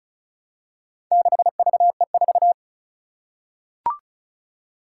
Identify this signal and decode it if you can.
Morse code, heard as a keyed tone, about 36 wpm, 700 Hz. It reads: 6VE4
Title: Four Challenging Letter-Number Combinations 35wpm